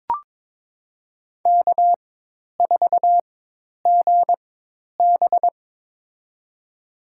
Morse code, heard as a keyed tone, about 21 wpm, spaced as for 10 wpm, 700 Hz. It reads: K4GB
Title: USA Call Signs - 4 Characters - 4x Character Spacing - 22wpm